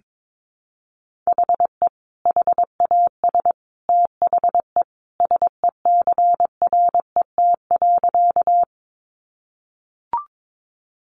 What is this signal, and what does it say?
Morse code, heard as a keyed tone, about 22 wpm, 700 Hz. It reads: HE HAS THE SECRET.
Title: Sentences from Top 1200 Words 22wpm